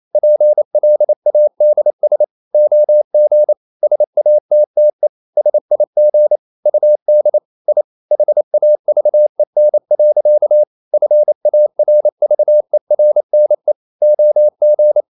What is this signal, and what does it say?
Morse code, heard as a keyed tone, about 28 wpm, 600 Hz. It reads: PLADS OG SATTE SIG UD I HAVEN. FARVERNE OG
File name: Ferie QRQ 28wpm